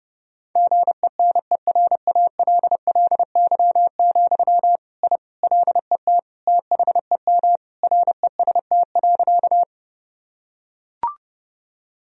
Morse code, heard as a keyed tone, about 30 wpm, 700 Hz. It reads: GENERALLY, I LET THEM REST.
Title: Sentences from Top 1000 Words - Review Part 4 - 30wpm